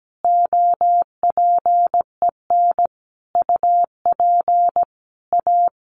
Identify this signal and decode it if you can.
Morse code, heard as a keyed tone, about 17 wpm, 700 Hz. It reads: OPEN UP A
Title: Additional Easy Phrases 17wpm